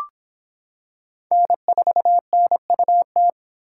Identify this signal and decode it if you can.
Morse code, heard as a keyed tone, about 26 wpm, 700 Hz. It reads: N4NUT